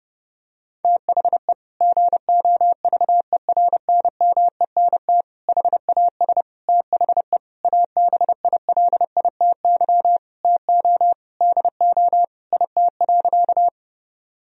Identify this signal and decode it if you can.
Morse code, heard as a keyed tone, about 32 wpm, 700 Hz. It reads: THE GOVERNMENT HAS THE ABILITY TO DO IT.
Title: Sentences from Top 1000 Words - Review Part 3 - 30wpm